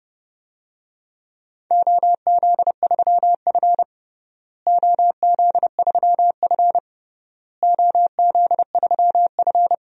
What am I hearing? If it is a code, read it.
Morse code, heard as a keyed tone, about 29 wpm, 700 Hz. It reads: OZ3F OZ3F OZ3F
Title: DXCC-100 Call Signs - 4 Characters - Repeated 3x - 30wpm